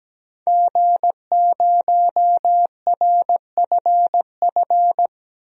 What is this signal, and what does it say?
Morse code, heard as a keyed tone, about 17 wpm, 700 Hz. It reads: G0RFF
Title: DXCC-100 Call Signs - 5 Characters - 17wpm